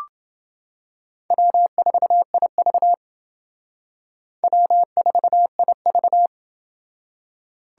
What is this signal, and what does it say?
Morse code, heard as a keyed tone, about 28 wpm, 700 Hz. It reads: W4IV W4IV